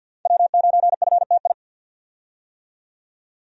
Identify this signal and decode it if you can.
Morse code, heard as a keyed tone, about 50 wpm, 700 Hz. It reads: W9FTI
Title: USA Call Signs - 5 Characters - 50wpm